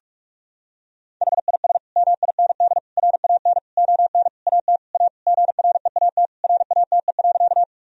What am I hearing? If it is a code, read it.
Morse code, heard as a keyed tone, about 41 wpm, 700 Hz. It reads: HIS MIND RAN ON AT A GREAT RATE.
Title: Sentences from Top 1000 Words - Review Part 4 - 45wpm